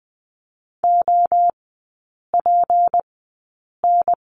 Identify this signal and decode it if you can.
Morse code, heard as a keyed tone, about 20 wpm, spaced as for 8 wpm, 690 Hz. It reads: OPN
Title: 159 Lesson 26e - Letter g - Three Characters 20wpm